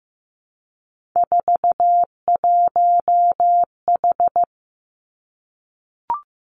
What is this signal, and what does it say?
Morse code, heard as a keyed tone, about 15 wpm, 700 Hz. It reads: 41H
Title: Three Challenging Letter-Number Combinations 15wpm